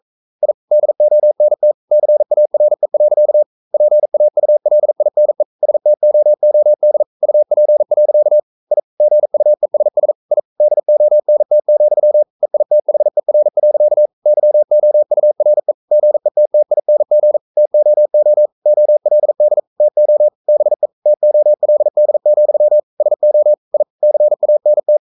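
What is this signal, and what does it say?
Morse code, heard as a keyed tone, about 42 wpm, 600 Hz. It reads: I DONT CARE. PAULINE STOOD UP. I GUESS I DONT, EITHER. YOURE GETTING TOO OLD TO BE TOLD, SO I CANT
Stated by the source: Morse code sending of an English novel at 42 wpm.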